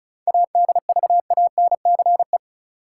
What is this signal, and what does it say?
Morse code, heard as a keyed tone, about 34 wpm, 700 Hz. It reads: ADVANCE
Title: Sentences from Top 1400 Words - New Word Drill Alt - 35wpm